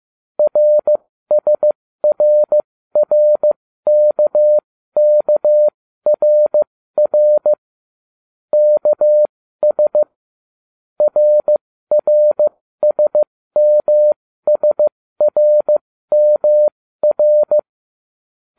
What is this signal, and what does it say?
Morse code, heard as a keyed tone, about 15 wpm, spaced as for 13 wpm, 600 Hz. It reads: RSRRKKRR KS RRSMSRMR